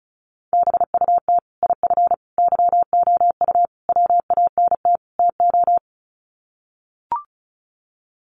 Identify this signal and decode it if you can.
Morse code, heard as a keyed tone, about 34 wpm, 700 Hz. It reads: BUT IF YOU WANT TO
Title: Sets of 5 Words Encore2 35wpm